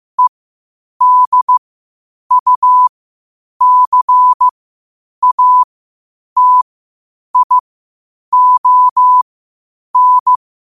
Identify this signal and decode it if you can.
Morse code, heard as a keyed tone, about 15 wpm, spaced as for 8 wpm, 1000 Hz. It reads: EDUCATION
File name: qotdfiles / 8wpm